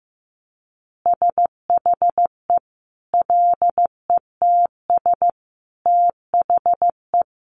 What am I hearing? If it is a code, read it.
Morse code, heard as a keyed tone, about 15 wpm, 700 Hz. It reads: SHE LETS THE
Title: Additional Easy Phrases 15wpm